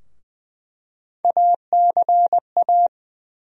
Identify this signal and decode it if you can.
Morse code, heard as a keyed tone, about 20 wpm, 700 Hz. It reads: ACA
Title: USA Call Sign Postfixes - 3 Letters 20wpm